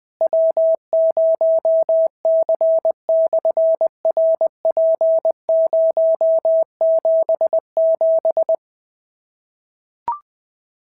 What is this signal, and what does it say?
Morse code, heard as a keyed tone, about 20 wpm, 650 Hz. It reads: W0C/RP077
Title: SOTA QSO Practice 20wpm